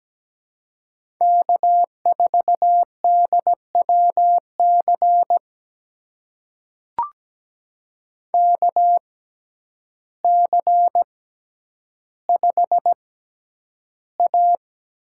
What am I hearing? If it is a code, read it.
Morse code, heard as a keyed tone, about 17 wpm, spaced as for 9 wpm, 700 Hz. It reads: K4DWC KC5A
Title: USA Call Signs - 5 Characters - 6x Character Spacing - 17wpm